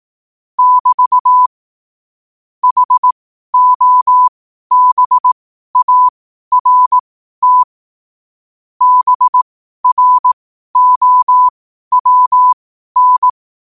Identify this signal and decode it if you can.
Morse code, heard as a keyed tone, about 18 wpm, spaced as for 12 wpm, 1000 Hz. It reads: <BT> HOBART BROWN
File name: qotdfiles / 12wpm